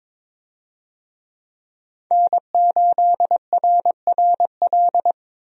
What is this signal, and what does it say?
Morse code, heard as a keyed tone, about 21 wpm, 700 Hz. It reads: N8RRL
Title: USA Call Signs - 5 Characters - Repeated 2x - 22wpm